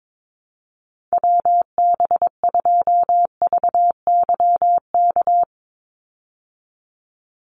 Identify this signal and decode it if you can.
Morse code, heard as a keyed tone, about 22 wpm, 700 Hz. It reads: WB2VYK
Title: Call Signs Repeated 2x - 22wpm